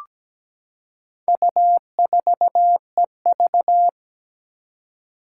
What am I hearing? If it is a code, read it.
Morse code, heard as a keyed tone, about 17 wpm, 700 Hz. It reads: U4EV
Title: Four Challenging Letter-Number Combinations 17wpm